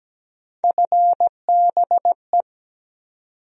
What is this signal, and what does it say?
Morse code, heard as a keyed tone, about 17 wpm, 700 Hz. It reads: FBE
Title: USA Call Sign Postfixes - 3 Letters 17wpm